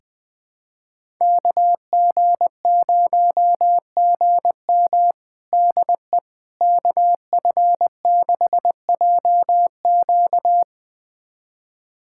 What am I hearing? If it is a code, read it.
Morse code, heard as a keyed tone, about 21 wpm, 700 Hz. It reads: KG0GM DE KF6JQ
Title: USA Callsign DE Callsign - 5 Characters - 20wpm